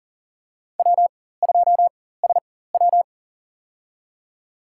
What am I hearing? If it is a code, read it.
Morse code, heard as a keyed tone, about 40 wpm, spaced as for 19 wpm, 700 Hz. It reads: W2SW
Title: USA Call Signs - 4 Characters - 4x Character Spacing - 40wpm